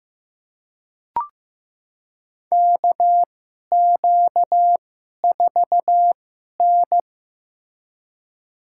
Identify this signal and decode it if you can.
Morse code, heard as a keyed tone, about 15 wpm, spaced as for 11 wpm, 700 Hz. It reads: KQ4N
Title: USA Call Signs - 4 Characters - 2x Character Spacing - 15wpm